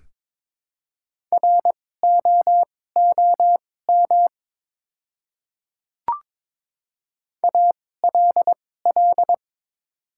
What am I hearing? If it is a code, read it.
Morse code, heard as a keyed tone, about 22 wpm, spaced as for 16 wpm, 700 Hz. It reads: ROOM ALL
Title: 2x Character Spacing - Top 300 Words - 22wpm